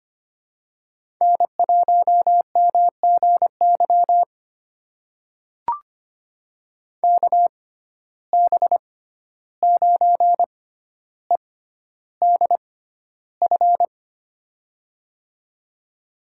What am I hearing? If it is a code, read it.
Morse code, heard as a keyed tone, about 25 wpm, spaced as for 12 wpm, 700 Hz. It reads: N1MGY KB9EDF
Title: USA Call Signs - 6x Character Spacing - 25wpm